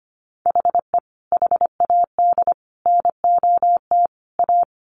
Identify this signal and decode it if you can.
Morse code, heard as a keyed tone, about 25 wpm, 700 Hz. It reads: HE HAD NOT A
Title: Sentences from Top 1100 Words 25wpm